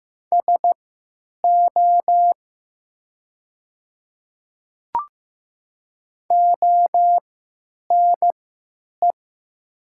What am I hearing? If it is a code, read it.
Morse code, heard as a keyed tone, about 15 wpm, spaced as for 9 wpm, 690 Hz. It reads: SO ONE
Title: 3x Character Spacing - Top 100 Words - 15wpm